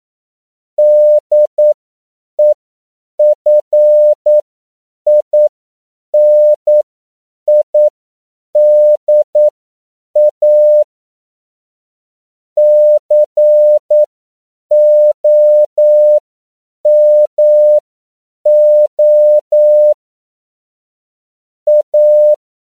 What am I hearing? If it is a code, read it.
Morse code, heard as a keyed tone, about 9 wpm, spaced as for 7 wpm, 600 Hz. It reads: DEFINIDA COMO A
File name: CW / 10PPM